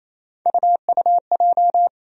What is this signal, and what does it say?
Morse code, heard as a keyed tone, about 28 wpm, 700 Hz. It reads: UUJ
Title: USA Call Sign Postfixes - 3 Letters 28wpm